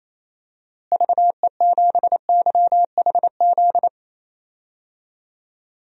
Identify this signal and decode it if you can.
Morse code, heard as a keyed tone, about 28 wpm, 700 Hz. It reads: VE7YHZ
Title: DXCC-100 Call Signs - 6 Characters - Repeated 3x - 28wpm